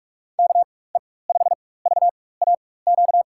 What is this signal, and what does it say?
Morse code, heard as a keyed tone, about 43 wpm, spaced as for 21 wpm, 700 Hz. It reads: KE5VAQ
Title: USA Call Signs - 6 Characters - 4x Character Spacing - 45wpm